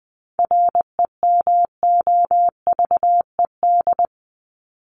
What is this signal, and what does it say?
Morse code, heard as a keyed tone, about 20 wpm, 700 Hz. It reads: REMOVED
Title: ED Suffix - Top 100 Words with Base Words 20wpm